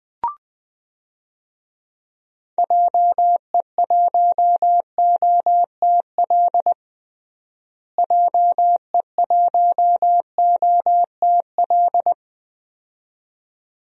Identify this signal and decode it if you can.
Morse code, heard as a keyed tone, about 20 wpm, 690 Hz. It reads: JE1OTL JE1OTL
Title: DXCC-100 Call Signs - 6 Characters - Repeated 2x - 20wpm